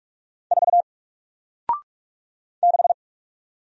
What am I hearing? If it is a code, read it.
Morse code, heard as a keyed tone, about 44 wpm, 700 Hz. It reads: 4 6
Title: Single Letter-Number Alt 45wpm